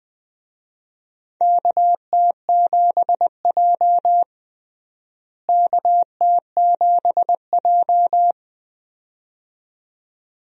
Morse code, heard as a keyed tone, about 20 wpm, 700 Hz. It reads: KT7J KT7J
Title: USA Call Signs - 4 Characters - Repeated 2x - 20wpm